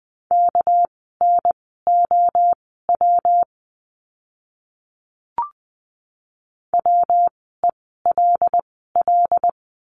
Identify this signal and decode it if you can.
Morse code, heard as a keyed tone, about 20 wpm, spaced as for 15 wpm, 700 Hz. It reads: KNOW WELL
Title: 244 Lesson 39f - Letter z - Words 20wpm